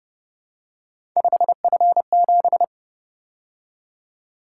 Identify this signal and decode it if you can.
Morse code, heard as a keyed tone, about 30 wpm, 700 Hz. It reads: 5F7
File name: Speed-Racing - Three Letter-Number Combinations - 25wpm